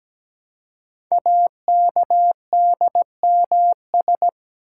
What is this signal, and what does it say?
Morse code, heard as a keyed tone, about 17 wpm, 700 Hz. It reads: AKDMS
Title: Code Groups - 5 Letters - 17wpm